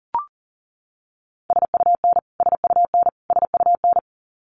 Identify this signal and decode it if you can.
Morse code, heard as a keyed tone, about 40 wpm, 700 Hz. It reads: SUN SUN SUN
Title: Top 300 Words Repeated 40wpm